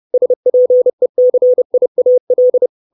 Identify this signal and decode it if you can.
Morse code, heard as a keyed tone, about 30 wpm, 500 Hz. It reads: SPECIAL
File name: 632 Words 30WPM